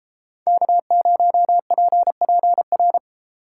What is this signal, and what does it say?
Morse code, heard as a keyed tone, about 32 wpm, 700 Hz. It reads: K0PPR